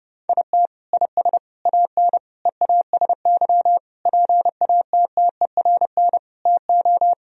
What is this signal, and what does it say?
Morse code, heard as a keyed tone, about 30 wpm, 700 Hz. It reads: IT IS AN EASY PATTERN TO
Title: Sentences from Top 1000 Words - Review Part 2 - 30wpm